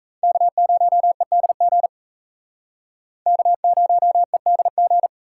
Morse code, heard as a keyed tone, about 35 wpm, 700 Hz. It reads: K0EDG K0EDG